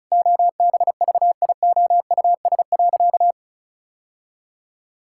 Morse code, heard as a keyed tone, about 34 wpm, 700 Hz. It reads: OBVIOUS.
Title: Sentences from Top 1600 Words 35wpm